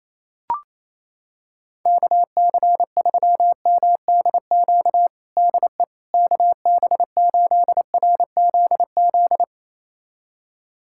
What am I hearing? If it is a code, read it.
Morse code, heard as a keyed tone, about 28 wpm, 700 Hz. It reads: KC3MDQ DE KB8RZZ
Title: USA Callsign DE Callsign 28wpm